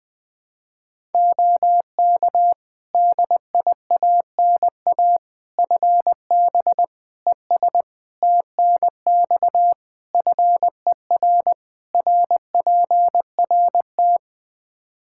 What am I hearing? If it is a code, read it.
Morse code, heard as a keyed tone, about 21 wpm, 700 Hz. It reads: OK DIANA FB ES TNX FER RPRT
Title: Practice QSO - Exchange 2 - Info - 20wpm